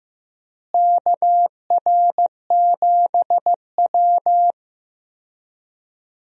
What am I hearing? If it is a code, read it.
Morse code, heard as a keyed tone, about 15 wpm, 700 Hz. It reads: KR7W
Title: USA Call Signs - 4 Characters - Repeated 3x - 15wpm